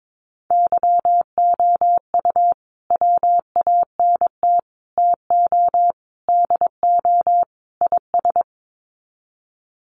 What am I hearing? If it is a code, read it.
Morse code, heard as a keyed tone, about 22 wpm, 700 Hz. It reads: YOU WANT TO DO IS
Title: Sets of 5 Words Encore2 22wpm